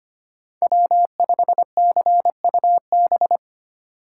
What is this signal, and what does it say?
Morse code, heard as a keyed tone, about 25 wpm, 700 Hz. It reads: W5CUB
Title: USA Call Signs - 4x Character Spacing - 25wpm